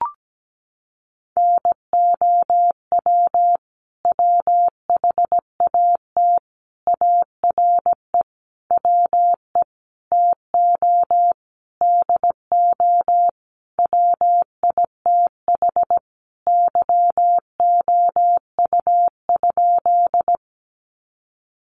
Morse code, heard as a keyed tone, about 17 wpm, 700 Hz. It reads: NOW WHAT ARE WE TO DO WITH YOU?
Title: Sentences from Top 100 Words 17wpm